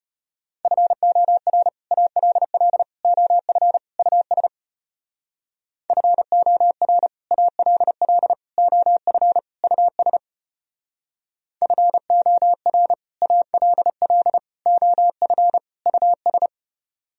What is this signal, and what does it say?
Morse code, heard as a keyed tone, about 30 wpm, 700 Hz. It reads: FOR ALL OF US FOR ALL OF US FOR ALL OF US
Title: Speed-Racing - Sets of 4 Words 25wpm